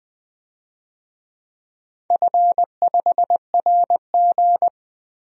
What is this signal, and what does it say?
Morse code, heard as a keyed tone, about 20 wpm, 700 Hz. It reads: F5RG
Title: DXCC-100 Call Signs - 4 Characters - Repeated 3x - 20wpm